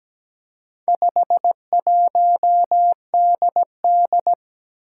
Morse code, heard as a keyed tone, about 17 wpm, 700 Hz. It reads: 51DD
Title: Four Challenging Letter-Number Combinations 17wpm